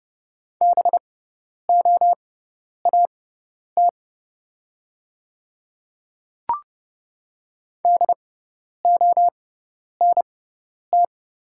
Morse code, heard as a keyed tone, about 30 wpm, spaced as for 10 wpm, 700 Hz. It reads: BOAT DONT
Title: 6x Character Spacing - Top 500 Words - 30wpm